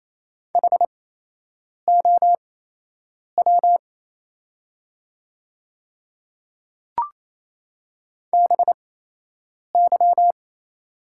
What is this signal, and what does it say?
Morse code, heard as a keyed tone, about 28 wpm, spaced as for 8 wpm, 700 Hz. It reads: HOW BY
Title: 8x Character Spacing - Top 300 Words - 28wpm